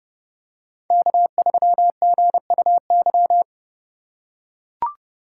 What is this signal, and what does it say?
Morse code, heard as a keyed tone, about 32 wpm, 700 Hz. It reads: K3GUY
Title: USA Call Signs - 3x Character Spacing - 30wpm